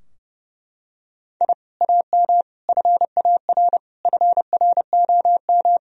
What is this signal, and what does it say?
Morse code, heard as a keyed tone, about 30 wpm, 700 Hz. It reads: I AM FAR FROM
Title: Sentences from Top 1000 Words - Review Part 3 - 30wpm